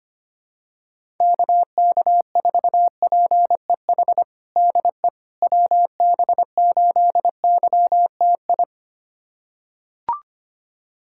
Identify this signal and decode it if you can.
Morse code, heard as a keyed tone, about 25 wpm, 700 Hz. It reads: KK4PEH DE WB8YTI
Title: USA Callsign DE Callsign - 6 Characters - 25wpm